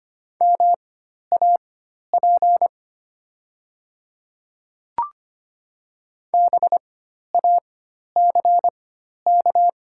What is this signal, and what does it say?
Morse code, heard as a keyed tone, about 25 wpm, spaced as for 12 wpm, 700 Hz. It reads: MAP BACK
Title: 4x Character Spacing - Top 500 Words - 25wpm